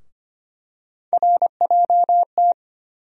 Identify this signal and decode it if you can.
Morse code, heard as a keyed tone, about 25 wpm, 700 Hz. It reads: RJT
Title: USA Call Sign Postfixes - 3 Letters 25wpm